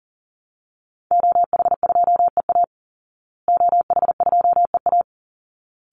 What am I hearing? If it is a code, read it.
Morse code, heard as a keyed tone, about 40 wpm, 700 Hz. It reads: OH2EA OH2EA
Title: Call Signs Repeated 3x - 40wpm